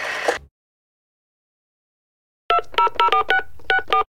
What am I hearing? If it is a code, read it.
DTMF, heard as telephone touch tones, tones of about 100 ms. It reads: 2004664